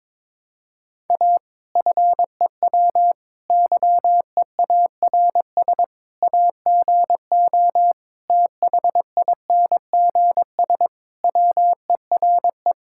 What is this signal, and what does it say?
Morse code, heard as a keyed tone, about 22 wpm, 700 Hz. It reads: A FEW YEARS AGO THINGS WERE
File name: practice / 22wpm / Sentences with 4-Word Phrases - 22wpm